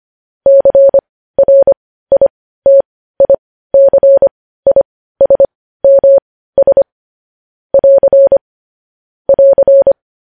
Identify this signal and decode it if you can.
Morse code, heard as a keyed tone, about 25 wpm, spaced as for 15 wpm, 550 Hz. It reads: CRITICISMS <AR> <AR>
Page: news headlines in morse code at 15 wpm